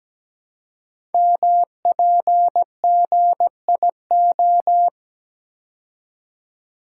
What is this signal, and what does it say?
Morse code, heard as a keyed tone, about 17 wpm, 700 Hz. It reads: MPGIO
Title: Code Groups - 5 Letters - 17wpm